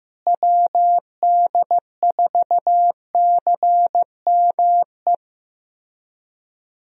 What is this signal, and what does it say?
Morse code, heard as a keyed tone, about 15 wpm, 700 Hz. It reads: WD4CME
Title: USA Call Signs - 6 Characters - Repeated 3x - 15wpm